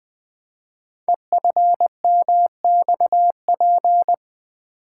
Morse code, heard as a keyed tone, about 20 wpm, 700 Hz. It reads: EFMXP
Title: Code Groups - 5 Letters - 20wpm